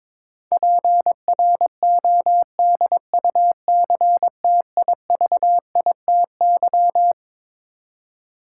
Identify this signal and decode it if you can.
Morse code, heard as a keyed tone, about 22 wpm, 700 Hz. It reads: PRODUCTIVITY
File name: Twelve Letter Words - 22wpm